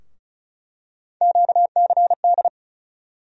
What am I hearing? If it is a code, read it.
Morse code, heard as a keyed tone, about 36 wpm, 700 Hz. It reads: QCD
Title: USA Call Sign Postfixes - 3 Letters 35wpm